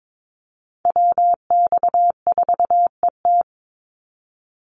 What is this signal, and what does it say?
Morse code, heard as a keyed tone, about 22 wpm, 700 Hz. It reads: WX4ET
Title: NA SOTA Chaser Callsigns 22wpm